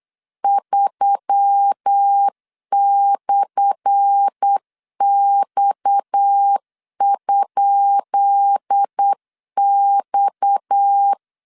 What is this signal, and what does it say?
Morse code, heard as a keyed tone, about 8 wpm, 800 Hz. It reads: 3/X?X
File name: LEZIONI / 10 SIGNS / 10B INT 8wpm 3534